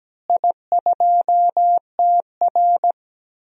Morse code, heard as a keyed tone, about 17 wpm, 700 Hz. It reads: I2TR
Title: DXCC-100 Call Signs - 4 Characters - Repeated 3x - 17wpm